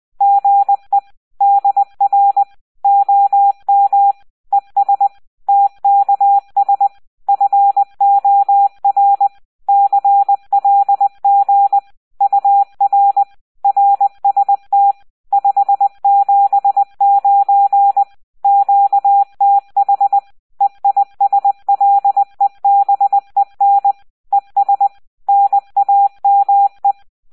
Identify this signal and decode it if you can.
Morse code, heard as a keyed tone, about 20 wpm, 800 Hz, with radio noise. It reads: GE DR OM ES TKS FOR CLG UR RST 579 QTH EISLEBEN ES NAME